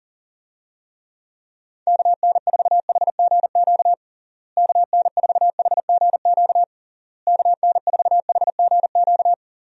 Morse code, heard as a keyed tone, about 40 wpm, 700 Hz. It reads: KN4HGQ KN4HGQ KN4HGQ
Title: USA Call Signs - 6 Characters - Repeated 3x - 40wpm